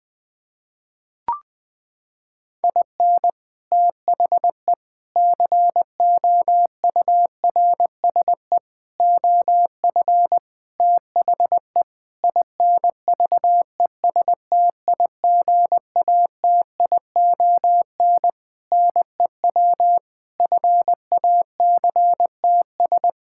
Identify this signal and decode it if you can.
Morse code, heard as a keyed tone, about 20 wpm, 700 Hz. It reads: IN THE COURSE OF THE INVESTIGATION NEW FACTS
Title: Sentences with 5-Word Phrases - 20wpm